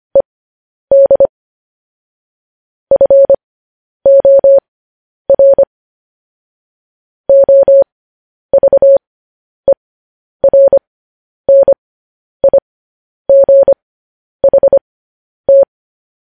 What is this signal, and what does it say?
Morse code, heard as a keyed tone, about 25 wpm, spaced as for 10 wpm, 550 Hz. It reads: ED FOR OVERNIGHT
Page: News Headlines in Morse Code at 10 WPM